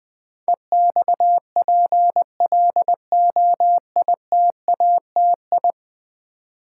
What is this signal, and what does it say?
Morse code, heard as a keyed tone, about 20 wpm, 700 Hz. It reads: EXPLOITATI
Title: Build A Word - 5 Letters At A Time - 12-Letter Words - 20wpm